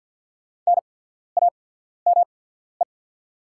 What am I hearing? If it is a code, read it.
Morse code, heard as a keyed tone, about 50 wpm, spaced as for 14 wpm, 690 Hz. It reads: NAME
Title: 8x Character Spacing - Top 200 Words - 50wpm